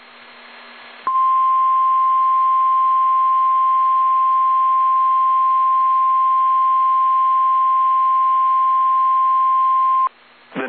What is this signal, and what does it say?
Emergency Alert System (SAME) tones (the 1050 Hz attention tone). It is heard from 1.0 to 10.1 s.
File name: eas / KSDZKDJL / KSDZKDJL